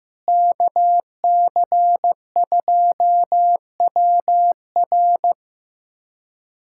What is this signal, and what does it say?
Morse code, heard as a keyed tone, about 15 wpm, 700 Hz. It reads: KC2WR
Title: USA Callsign DE Callsign - 5 Characters - 15wpm